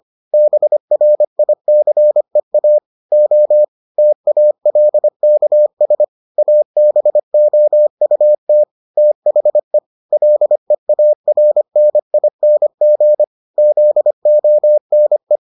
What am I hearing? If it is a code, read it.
Morse code, heard as a keyed tone, about 25 wpm, 600 Hz. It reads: BRICEA O TALKS ABOUT THE LEARNING ZONE